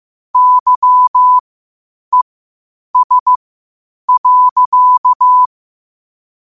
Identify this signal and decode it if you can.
Morse code, heard as a keyed tone, about 15 wpm, spaced as for 8 wpm, 1000 Hz. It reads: YES.
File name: qotdfiles / 8wpm